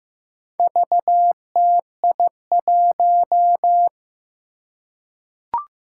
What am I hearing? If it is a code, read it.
Morse code, heard as a keyed tone, about 15 wpm, 700 Hz. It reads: VTI1
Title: Four Challenging Letter-Number Combinations 15wpm